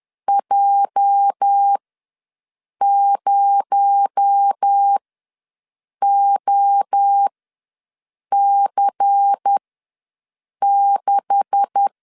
Morse code, heard as a keyed tone, about 10 wpm, spaced as for 6 wpm, 800 Hz. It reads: J0OC6
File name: LEZIONI / 9 286 ws 10wpm 4032